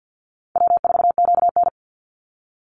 Morse code, heard as a keyed tone, about 50 wpm, 700 Hz. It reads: W4QN
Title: Easy Call Signs - 50wpm